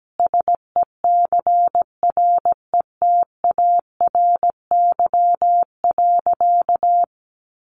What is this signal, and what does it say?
Morse code, heard as a keyed tone, about 17 wpm, 700 Hz. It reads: SECRETARY.
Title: Sentences from Top 1300 Words 17wpm